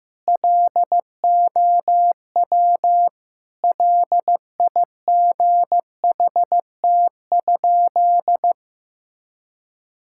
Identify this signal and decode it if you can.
Morse code, heard as a keyed tone, about 16 wpm, 700 Hz. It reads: LOW LIGHT?
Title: Sentences from Top 1000 Words - Review Part 5 - 15wpm